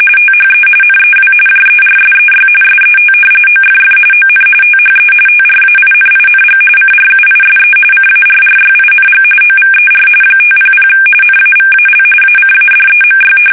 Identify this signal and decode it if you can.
RTTY, about 45.45 baud. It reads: THE NEW JERSEY SET OFF ON THE TRAINING CRUISE CARRYING OVER 500 NAVAL ACADEMY
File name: fsk850 / battleshipnj